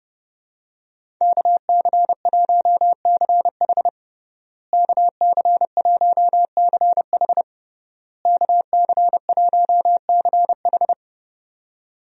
Morse code, heard as a keyed tone, about 32 wpm, 700 Hz. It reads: KC1CH KC1CH KC1CH
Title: USA Call Signs - 5 Characters - Repeated 3x - 30wpm